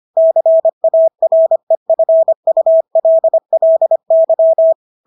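Morse code, heard as a keyed tone, about 25 wpm, 650 Hz. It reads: CAREFULLY
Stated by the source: Morse practice of English words at 25 wpm from CW med Gnister.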